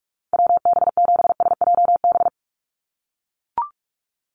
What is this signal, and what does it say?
Morse code, heard as a keyed tone, about 44 wpm, 700 Hz. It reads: WB7SJB
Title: USA Call Signs 45wpm